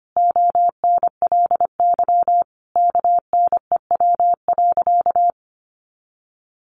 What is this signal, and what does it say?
Morse code, heard as a keyed tone, about 25 wpm, 700 Hz. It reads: ONLY KNEW.
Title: Sentences from Top 500 Words 25wpm